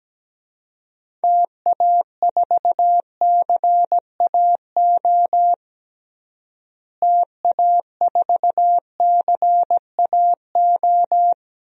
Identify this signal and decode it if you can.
Morse code, heard as a keyed tone, about 17 wpm, 700 Hz. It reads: TA4CAO TA4CAO
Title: DXCC-100 Call Signs - 6 Characters - Repeated 2x - 17wpm